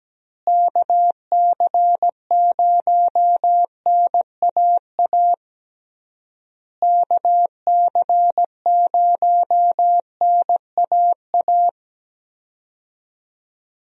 Morse code, heard as a keyed tone, about 17 wpm, 700 Hz. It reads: KC0NAA KC0NAA
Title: USA Call Signs - 6 Characters - Repeated 3x - 17wpm